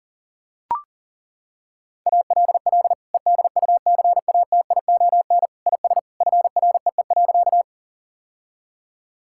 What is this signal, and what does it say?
Morse code, heard as a keyed tone, about 40 wpm, 700 Hz. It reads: ALL EDUCATION IS FREE.
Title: Sentences from Top 1000 Words - Review Part 6 - 40wpm